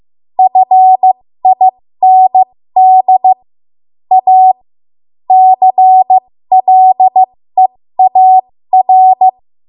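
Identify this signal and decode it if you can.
Morse code, heard as a keyed tone, about 15 wpm, spaced as for 13 wpm, 750 Hz. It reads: FIND A CLEAR
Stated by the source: 13 wpm Morse code practice archive transmission.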